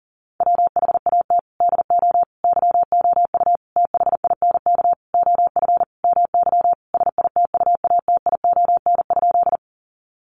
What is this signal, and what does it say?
Morse code, heard as a keyed tone, about 40 wpm, 700 Hz. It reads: WHAT DO YOU THINK OF MY SITUATION?
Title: Sentences from Top 1200 Words 40wpm